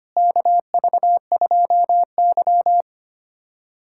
Morse code, heard as a keyed tone, about 25 wpm, 700 Hz. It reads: KV2Y
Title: USA Callsign DE Callsign - 4 Characters - 25wpm